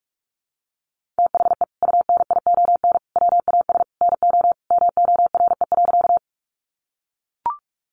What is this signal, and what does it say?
Morse code, heard as a keyed tone, about 44 wpm, 700 Hz. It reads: THE UNION WAS NO MORE.
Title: Sentences from Top 1400 Words 45wpm